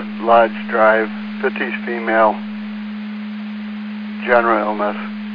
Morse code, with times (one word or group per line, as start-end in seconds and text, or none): none